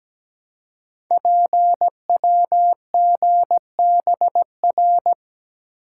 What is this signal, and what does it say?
Morse code, heard as a keyed tone, about 17 wpm, 700 Hz. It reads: PWGBR
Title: Code Groups - 5 Letters - 17wpm